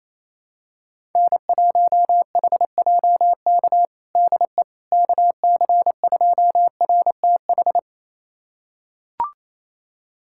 Morse code, heard as a keyed tone, about 28 wpm, 700 Hz. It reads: N1HJK DE KC2RTH
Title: USA Callsign DE Callsign 28wpm